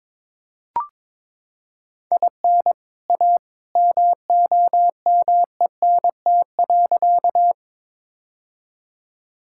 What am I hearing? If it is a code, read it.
Morse code, heard as a keyed tone, about 22 wpm, 700 Hz. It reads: IN A MOMENT.
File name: practice / 22wpm / Sentences with 2-Word Phrases - 22wpm